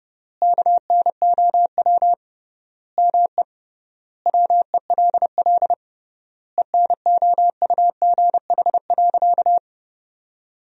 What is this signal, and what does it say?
Morse code, heard as a keyed tone, about 30 wpm, 700 Hz. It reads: KNOW ME WELL ENOUGH.
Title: 3x Word-Spacing - Sentences from Top 400 Words - 30wpm